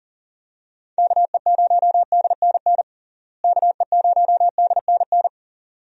Morse code, heard as a keyed tone, about 40 wpm, 700 Hz. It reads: KE0DNN KE0DNN
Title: USA Call Signs - 6 Characters - Repeated 2x - 40wpm